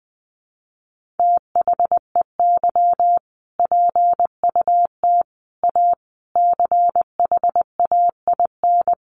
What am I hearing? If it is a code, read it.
Morse code, heard as a keyed tone, about 20 wpm, 700 Hz. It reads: THEY PUT A CHAIN
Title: Sentences from Top 1500 Words 20wpm